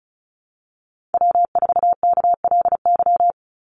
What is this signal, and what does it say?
Morse code, heard as a keyed tone, about 34 wpm, 700 Hz. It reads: W4KLY
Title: Call Signs Repeated 2x - 35wpm